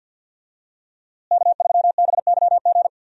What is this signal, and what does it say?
Morse code, heard as a keyed tone, about 52 wpm, 700 Hz. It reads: K3BYG
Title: USA Call Signs - 5 Characters - Repeated 3x - 50wpm